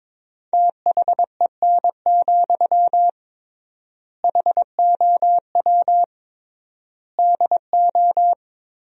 Morse code, heard as a keyed tone, about 22 wpm, 700 Hz. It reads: THEN, HOW DO
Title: 3x Word-Spacing - Sentences from Top 100 Words - 22wpm